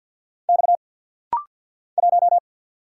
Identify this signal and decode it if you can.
Morse code, heard as a keyed tone, about 50 wpm, 700 Hz. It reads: X 1
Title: Single Letter-Number - Rapid-Fire 50wpm